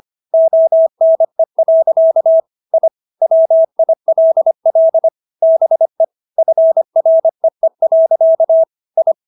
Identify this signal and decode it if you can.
Morse code, heard as a keyed tone, about 25 wpm, 650 Hz. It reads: ONE. I WILL BE FREE. I